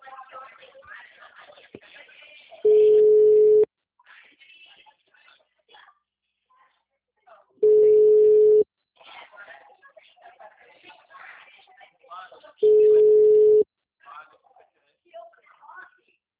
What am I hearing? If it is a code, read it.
Ringback tone (Europe, 425 Hz), with speech.